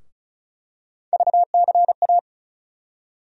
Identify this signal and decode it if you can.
Morse code, heard as a keyed tone, about 34 wpm, 700 Hz. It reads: VCA
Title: USA Call Sign Postfixes - 3 Letters 35wpm